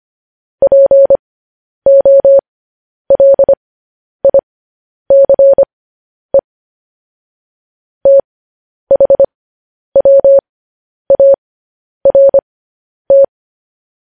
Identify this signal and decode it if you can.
Morse code, heard as a keyed tone, about 25 wpm, spaced as for 10 wpm, 550 Hz. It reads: POLICE THWART
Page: news-headlines-in-morse-code-at-10-wpm